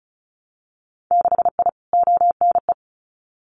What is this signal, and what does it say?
Morse code, heard as a keyed tone, about 36 wpm, 700 Hz. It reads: 6I ONE
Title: Winter Field Day Exchanges Encore 35wpm